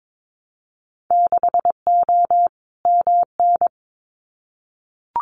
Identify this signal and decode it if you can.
Morse code, heard as a keyed tone, about 22 wpm, 700 Hz. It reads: 6O MN
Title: Winter Field Day Exchanges Encore 22wpm